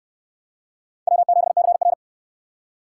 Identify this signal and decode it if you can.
Morse code, heard as a keyed tone, about 68 wpm, 700 Hz. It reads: W7CR